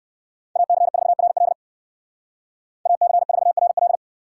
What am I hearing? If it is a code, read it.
Morse code, heard as a keyed tone, about 59 wpm, 690 Hz. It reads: AC4RL AC4RL